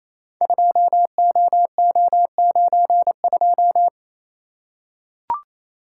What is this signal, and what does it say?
Morse code, heard as a keyed tone, about 29 wpm, 700 Hz. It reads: 2OO92
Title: Code Groups - 5 Letter:Numbers - 28wpm